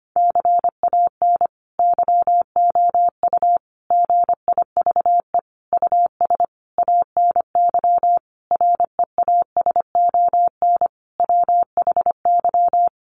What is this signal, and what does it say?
Morse code, heard as a keyed tone, about 25 wpm, 700 Hz. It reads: CAN YOU GIVE US ANY REASON WHY
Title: Sentences from Top 1100 Words 25wpm